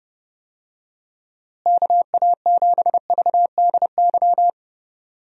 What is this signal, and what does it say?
Morse code, heard as a keyed tone, about 31 wpm, 700 Hz. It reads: KA7VDY
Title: USA Call Signs - 6 Characters - Repeated 2x - 30wpm